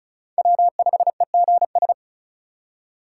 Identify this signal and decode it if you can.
Morse code, heard as a keyed tone, about 46 wpm, 700 Hz. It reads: W5EGS